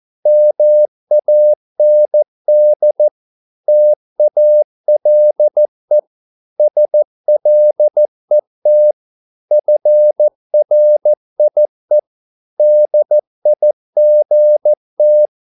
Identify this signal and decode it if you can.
Morse code, heard as a keyed tone, about 14 wpm, 600 Hz. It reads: MAND TALE SLET FRIE DIGT
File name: Korte ord DK 14wpm